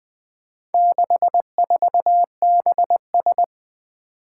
Morse code, heard as a keyed tone, about 20 wpm, 700 Hz. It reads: 64BS
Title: Four Challenging Letter-Number Combinations 20wpm